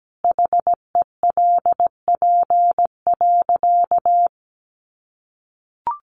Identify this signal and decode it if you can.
Morse code, heard as a keyed tone, about 17 wpm, 700 Hz. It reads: HELP.
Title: Sentences from Top 1100 Words 17wpm